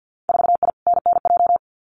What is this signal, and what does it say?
Morse code, heard as a keyed tone, about 50 wpm, 700 Hz. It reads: 4I NNJ
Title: Winter Field Day Exchanges 50wpm